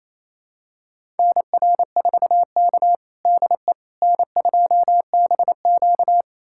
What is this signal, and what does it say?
Morse code, heard as a keyed tone, about 28 wpm, 700 Hz. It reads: NR4K DE N2BQ
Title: USA Callsign DE Callsign - 4 Characters - 28wpm